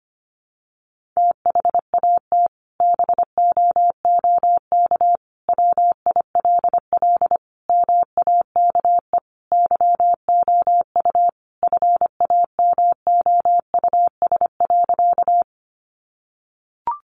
Morse code, heard as a keyed tone, about 25 wpm, 700 Hz. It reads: THAT BOOK WILL MAKE YOU FAMOUS.
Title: Sentences from Top 1500 Words 25wpm